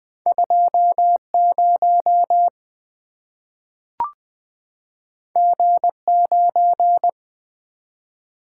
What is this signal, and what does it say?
Morse code, heard as a keyed tone, about 20 wpm, 700 Hz. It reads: 20 G9
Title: Two Letter-Number Combinations - 20wpm